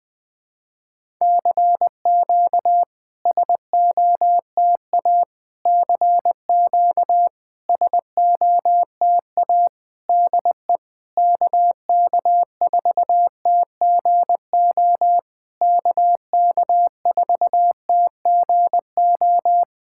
Morse code, heard as a keyed tone, about 20 wpm, 690 Hz. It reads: CQ SOTA CQ SOTA DE KK4TGO KK4TGO
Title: SOTA QSO Practice 20wpm